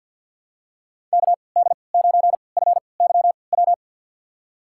Morse code, heard as a keyed tone, about 53 wpm, spaced as for 29 wpm, 700 Hz. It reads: KD9FYW
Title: USA Call Signs - 6 Characters - 3x Character Spacing - 50wpm